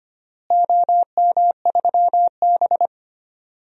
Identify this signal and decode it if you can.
Morse code, heard as a keyed tone, about 25 wpm, 700 Hz. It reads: OM3B
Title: DXCC-100 Call Signs - 4 Characters - Repeated 2x - 25wpm